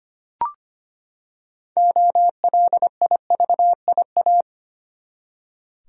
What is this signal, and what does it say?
Morse code, heard as a keyed tone, about 25 wpm, 700 Hz. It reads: OLIVIA
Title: USA Common Names - 25wpm